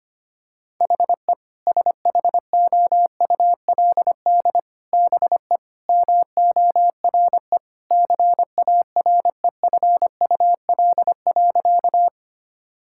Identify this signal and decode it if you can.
Morse code, heard as a keyed tone, about 25 wpm, 700 Hz. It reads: HE SHOULD BE MORE CAREFUL.
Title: Sentences from Top 1600 Words 25wpm